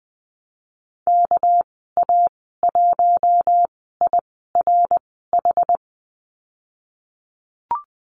Morse code, heard as a keyed tone, about 20 wpm, spaced as for 15 wpm, 700 Hz. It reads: KA1IRH
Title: 200 Lesson 32g - Letter k - Call Signs 20wpm